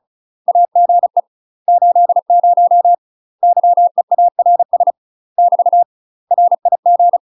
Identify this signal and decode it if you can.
Morse code, heard as a keyed tone, about 36 wpm, 700 Hz. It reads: AGE 80 YEARS <BT> RIG
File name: UK MASTER QSOs TRAINING / 35WPM / UK MASTER QSOs 0018